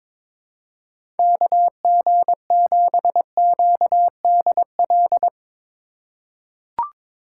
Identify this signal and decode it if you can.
Morse code, heard as a keyed tone, about 22 wpm, 700 Hz. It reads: KG7QDL
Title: USA Call Signs - 6 Characters - 2x Character Spacing - 22wpm